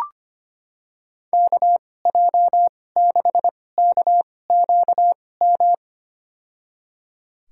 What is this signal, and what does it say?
Morse code, heard as a keyed tone, about 25 wpm, spaced as for 18 wpm, 700 Hz. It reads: KJ6KQM
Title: USA Call Signs - 6 Characters - 2x Character Spacing - 25wpm